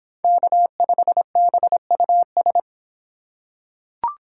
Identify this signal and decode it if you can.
Morse code, heard as a keyed tone, about 26 wpm, 700 Hz. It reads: K5BUS